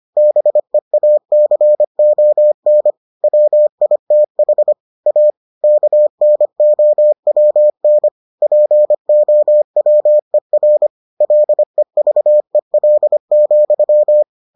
Morse code, heard as a keyed tone, about 25 wpm, 600 Hz. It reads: BEACON WITH A KNOWN POWER LEVEL,